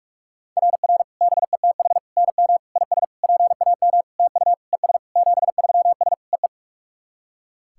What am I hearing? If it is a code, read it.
Morse code, heard as a keyed tone, about 44 wpm, 690 Hz. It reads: RR BETH NM IS PAM TU ES 73S EE
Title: SOTA QSO Practice 45wpm